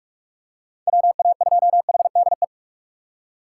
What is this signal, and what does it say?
Morse code, heard as a keyed tone, about 43 wpm, 700 Hz. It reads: WA2HDE
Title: USA Call Signs - 6 Characters - 4x Character Spacing - 45wpm